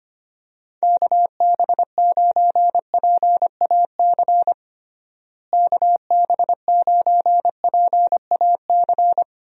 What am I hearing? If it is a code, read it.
Morse code, heard as a keyed tone, about 25 wpm, 700 Hz. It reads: KB9PAC KB9PAC
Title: USA Call Signs - 6 Characters - Repeated 2x - 25wpm